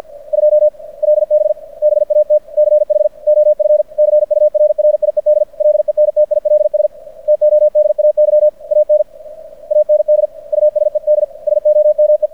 Morse code, heard as a keyed tone, about 50 wpm, 600 Hz. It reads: 2 LB BATCH OF CURRIED LENTILS TODAY AN AND USED SOME
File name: archived / CRSnet